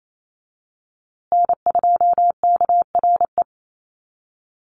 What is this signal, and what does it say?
Morse code, heard as a keyed tone, about 28 wpm, 700 Hz. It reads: N2KRE
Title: USA Call Signs Repeated 2x 28wpm